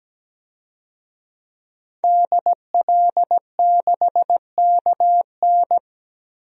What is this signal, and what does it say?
Morse code, heard as a keyed tone, about 17 wpm, 700 Hz. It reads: DL6KN
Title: DXCC-100 Call Signs - 5 Characters - 17wpm